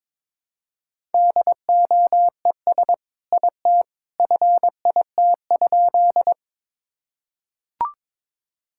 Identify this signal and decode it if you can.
Morse code, heard as a keyed tone, about 22 wpm, 700 Hz. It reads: DOES IT FIT?
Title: Sentences from Top 1000 Words - Review Part 5 - 22wpm